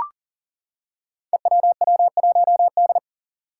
Morse code, heard as a keyed tone, about 40 wpm, 700 Hz. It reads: EWW1D
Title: Five Challenging Letter-Number Combinations 40wpm